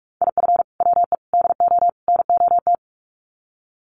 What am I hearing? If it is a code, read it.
Morse code, heard as a keyed tone, about 44 wpm, 700 Hz. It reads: IF WE DO NOT
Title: Sets of 4 Words 45wpm